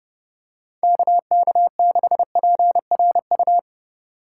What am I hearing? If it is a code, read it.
Morse code, heard as a keyed tone, about 30 wpm, 700 Hz. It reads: KK6PRU